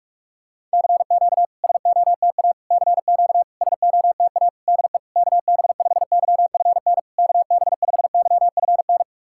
Morse code, heard as a keyed tone, about 44 wpm, 690 Hz. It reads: CQ SOTA CQ SOTA DE KB5YFN KB5YFN
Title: SOTA QSO Practice 45wpm